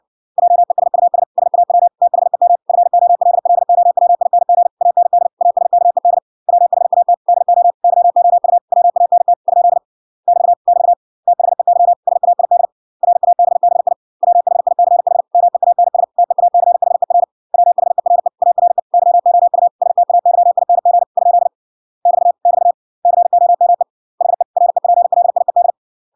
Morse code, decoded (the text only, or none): JESUS SAW THEM FOLLOWING AND ASKED WHAT DO YOU WANT ? <BT> <BT> THEY SAID RABBI WHICH MEANS TEACHER WHERE ARE YOU STAYING ? <BT> <BT> COME HE REPLIED